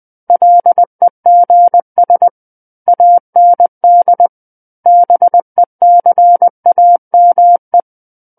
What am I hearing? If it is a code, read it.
Morse code, heard as a keyed tone, about 20 wpm, 700 Hz. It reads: LEGS AND BECAME
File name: 20 WPM